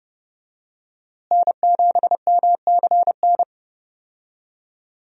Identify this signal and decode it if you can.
Morse code, heard as a keyed tone, about 30 wpm, 700 Hz. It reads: N7MCN
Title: USA Call Signs - 5 Characters - Repeated 2x - 30wpm